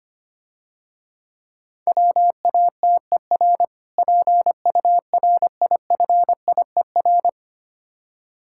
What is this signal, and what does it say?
Morse code, heard as a keyed tone, about 25 wpm, 700 Hz. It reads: WATER PURIFIER
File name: Easy Words - Speed-Racing with Hints - 25wpm